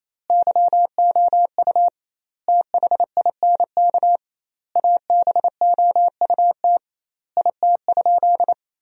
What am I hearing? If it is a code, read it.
Morse code, heard as a keyed tone, about 28 wpm, 700 Hz. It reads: YOU THINK ABOUT IT?
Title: 2x Word-Spacing - Sentences from Top 200 Words - 28wpm